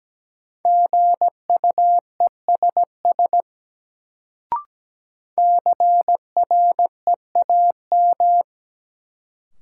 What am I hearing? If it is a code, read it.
Morse code, heard as a keyed tone, about 17 wpm, 700 Hz. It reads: GUESS CREAM
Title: Sentences from Top 1500 Words - New Word Drill - 17wpm